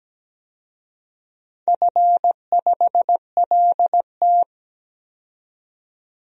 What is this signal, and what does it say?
Morse code, heard as a keyed tone, about 17 wpm, 700 Hz. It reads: F5LT
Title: DXCC-100 Call Signs - 4 Characters - 17wpm